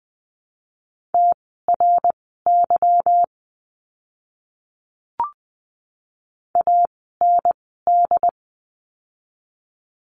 Morse code, heard as a keyed tone, about 20 wpm, spaced as for 15 wpm, 700 Hz. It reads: TRY AND
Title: 160 Lesson 26f - Letter g - Words 20wpm